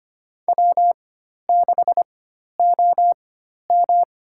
Morse code, heard as a keyed tone, about 25 wpm, spaced as for 12 wpm, 700 Hz. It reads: W6OM
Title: USA Call Signs - 4 Characters - 4x Character Spacing - 25wpm